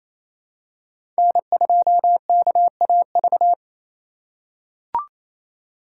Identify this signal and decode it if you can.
Morse code, heard as a keyed tone, about 27 wpm, 700 Hz. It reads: N2KAV
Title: USA Call Signs - 5 Characters - 2x Character Spacing - 28wpm